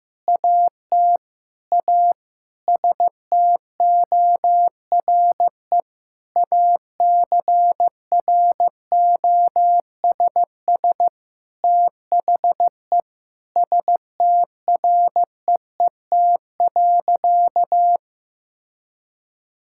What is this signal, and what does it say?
Morse code, heard as a keyed tone, about 15 wpm, 700 Hz. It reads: AT A STORE ACROSS THE STREET.
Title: Sentences from Top 1000 Words - Review Part 5 - 15wpm